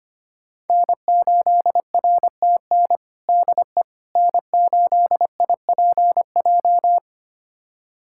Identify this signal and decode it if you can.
Morse code, heard as a keyed tone, about 25 wpm, 700 Hz. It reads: N8RTN DE N8IPJ
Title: USA Callsign DE Callsign - 5 Characters - 25wpm